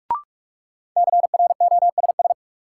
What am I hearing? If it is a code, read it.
Morse code, heard as a keyed tone, about 46 wpm, 700 Hz. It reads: CROSS
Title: Sentences from Top 300 Words - New Word Drill - 45wpm